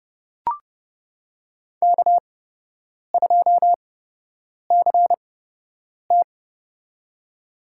Morse code, heard as a keyed tone, about 30 wpm, spaced as for 8 wpm, 700 Hz. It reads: K2CT
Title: USA Call Signs - 4 Characters - 8x Character Spacing - 30wpm